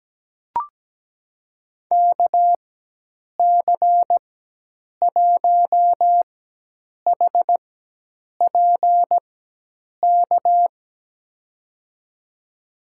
Morse code, heard as a keyed tone, about 17 wpm, spaced as for 8 wpm, 700 Hz. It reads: KC1HPK
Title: USA Call Signs - 6 Characters - 4x Character Spacing - 17wpm